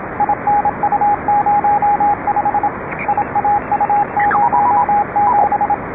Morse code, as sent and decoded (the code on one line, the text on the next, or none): .. -. ..- ----- ..... / .. .- ..- ----- --...
INU05 IAU07